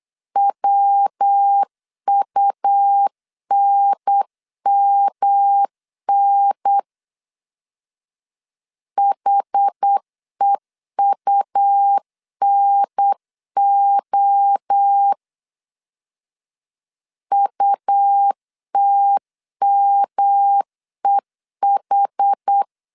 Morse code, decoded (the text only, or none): WUNMN HEUNO UTMEH